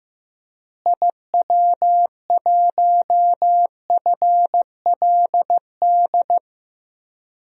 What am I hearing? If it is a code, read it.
Morse code, heard as a keyed tone, about 16 wpm, 700 Hz. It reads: IW1FLD
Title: DXCC-100 Call Signs - 6 Characters - Repeated 3x - 15wpm